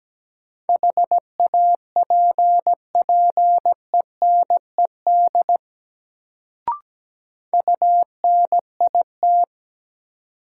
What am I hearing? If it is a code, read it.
Morse code, heard as a keyed tone, about 17 wpm, 700 Hz. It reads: HAPPENED UNIT
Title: Sentences from Top 1000 Words - Review Part 2 - New Word Drill - 17wpm